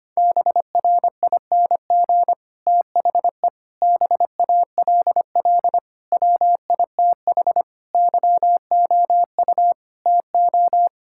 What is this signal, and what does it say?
Morse code, heard as a keyed tone, about 25 wpm, 700 Hz. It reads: BRING THE BALL WITH YOU TO
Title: Sentences from Top 1000 Words - Review Part 6 - 25wpm